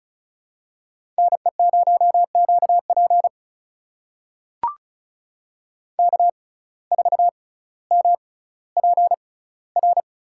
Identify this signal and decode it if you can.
Morse code, heard as a keyed tone, about 36 wpm, spaced as for 18 wpm, 700 Hz. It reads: NE0QP K4MPR
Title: USA Call Signs - 5 Characters - 6x Character Spacing - 35wpm